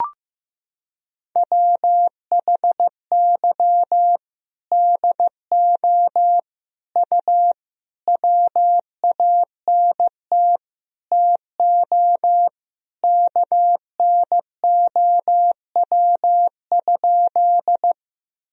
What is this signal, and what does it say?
Morse code, heard as a keyed tone, about 15 wpm, 700 Hz. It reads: WHY DO U WANT TO KNOW?
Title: Sentences from Top 200 Words and Abbreviations 15wpm